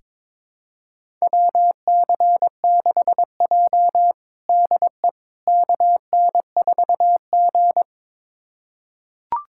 Morse code, heard as a keyed tone, about 23 wpm, 700 Hz. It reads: WC6J DE KN4G
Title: USA Callsign DE Callsign - 4 Characters - 22wpm